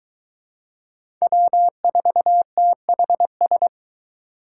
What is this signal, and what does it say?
Morse code, heard as a keyed tone, about 21 wpm, 700 Hz. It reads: W4THS